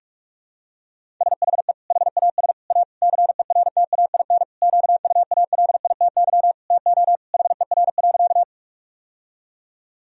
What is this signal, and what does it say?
Morse code, heard as a keyed tone, about 41 wpm, 700 Hz. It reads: SHE HAS A CERTAIN QUALITY TO HER.
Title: Sentences from Top 1000 Words - Review Part 4 - 45wpm